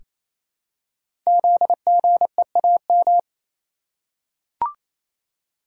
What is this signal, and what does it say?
Morse code, heard as a keyed tone, about 28 wpm, 700 Hz. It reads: ZGEAM
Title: Code Groups - 5 Letters - 28wpm